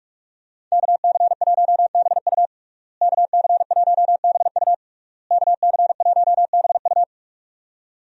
Morse code, heard as a keyed tone, about 47 wpm, 700 Hz. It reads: KC1BU KC1BU KC1BU
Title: USA Call Signs - 5 Characters - Repeated 3x - 45wpm